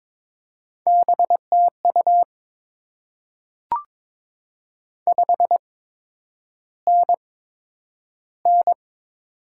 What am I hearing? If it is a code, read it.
Morse code, heard as a keyed tone, about 22 wpm, spaced as for 9 wpm, 700 Hz. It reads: BTU 5NN
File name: Top 20 QSO Elements - 8x Character Spacing - 22wpm